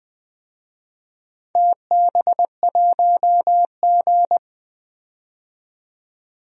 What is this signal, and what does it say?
Morse code, heard as a keyed tone, about 20 wpm, 700 Hz. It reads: TB1G
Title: DXCC-100 Call Signs - 4 Characters - 20wpm